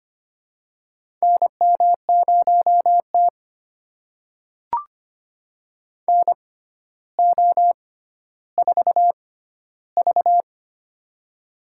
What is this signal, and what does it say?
Morse code, heard as a keyed tone, about 25 wpm, spaced as for 13 wpm, 700 Hz. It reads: NM0T NO4V
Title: USA Call Signs - 4 Characters - 6x Character Spacing - 25wpm